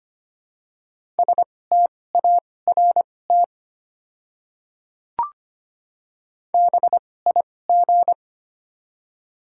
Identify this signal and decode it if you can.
Morse code, heard as a keyed tone, about 25 wpm, spaced as for 18 wpm, 700 Hz. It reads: START BIG
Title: 2x Character Spacing - Top 300 Words - 25wpm